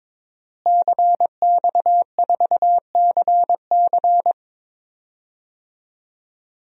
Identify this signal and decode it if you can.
Morse code, heard as a keyed tone, about 22 wpm, 700 Hz. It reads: CX4CC
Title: DXCC-100 Call Signs - 5 Characters - Repeated 2x - 22wpm